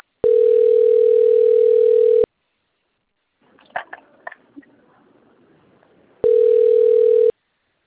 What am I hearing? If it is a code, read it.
Ringback tone (North America), 440 + 480 Hz.